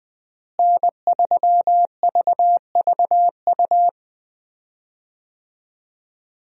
Morse code, heard as a keyed tone, about 21 wpm, 700 Hz. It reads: N3VVU
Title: USA Call Signs - 5 Characters - Repeated 3x - 20wpm